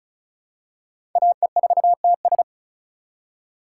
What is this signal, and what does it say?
Morse code, heard as a keyed tone, about 36 wpm, 700 Hz. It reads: AE4TS
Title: USA Call Signs - 5 Characters - 3x Character Spacing - 35wpm